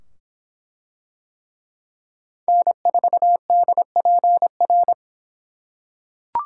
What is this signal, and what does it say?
Morse code, heard as a keyed tone, about 23 wpm, 700 Hz. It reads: N4DPR